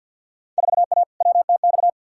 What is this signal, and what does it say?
Morse code, heard as a keyed tone, about 52 wpm, 700 Hz. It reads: 4A WTX
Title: Field Day Exchanges - 50wpm